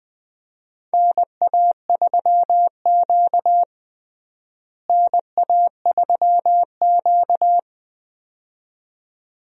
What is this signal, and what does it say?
Morse code, heard as a keyed tone, about 20 wpm, 700 Hz. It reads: NA3Q NA3Q
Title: USA Call Signs - 4 Characters - Repeated 2x - 20wpm